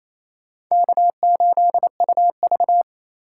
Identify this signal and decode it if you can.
Morse code, heard as a keyed tone, about 28 wpm, 700 Hz. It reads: K8UV
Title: USA Call Signs - 4 Characters - 8x Character Spacing - 28wpm